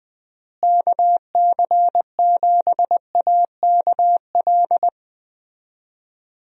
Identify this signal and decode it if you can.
Morse code, heard as a keyed tone, about 21 wpm, 700 Hz. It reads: KC7AKL
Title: USA Call Signs - 6 Characters - Repeated 2x - 20wpm